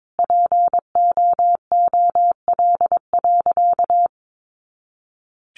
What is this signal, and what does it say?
Morse code, heard as a keyed tone, about 22 wpm, 700 Hz. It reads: POOL.
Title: Easy Phrases 22wpm